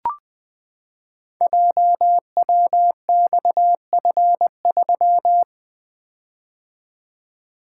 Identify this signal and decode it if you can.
Morse code, heard as a keyed tone, about 20 wpm, 700 Hz. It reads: JWXF3
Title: Code Groups - 5 Letter:Numbers - 20wpm